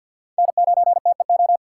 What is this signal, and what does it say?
Morse code, heard as a keyed tone, about 48 wpm, 700 Hz. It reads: N9TEO
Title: USA Call Signs - 5 Characters - 3x Character Spacing - 50wpm